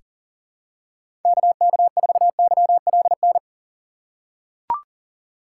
Morse code, heard as a keyed tone, about 38 wpm, 700 Hz. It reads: KK4YLN
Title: USA Call Signs - 6x Character Spacing - 40wpm